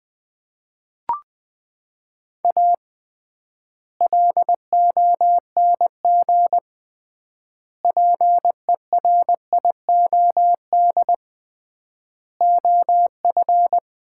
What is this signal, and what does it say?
Morse code, heard as a keyed tone, about 20 wpm, 700 Hz. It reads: A LONG PERIOD OF
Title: 3x Word-Spacing - Sets of 5 Words - 20wpm